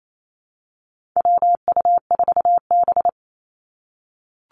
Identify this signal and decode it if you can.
Morse code, heard as a keyed tone, about 28 wpm, 700 Hz. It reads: WU4B
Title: Call Signs - 28wpm